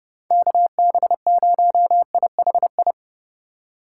Morse code, heard as a keyed tone, about 30 wpm, 700 Hz. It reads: KB0IHI
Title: USA Call Signs - 6 Characters - 4x Character Spacing - 30wpm